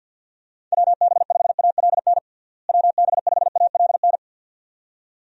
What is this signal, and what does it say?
Morse code, heard as a keyed tone, about 52 wpm, 700 Hz. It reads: WB5ALN WB5ALN
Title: USA Call Signs - 6 Characters - Repeated 2x - 50wpm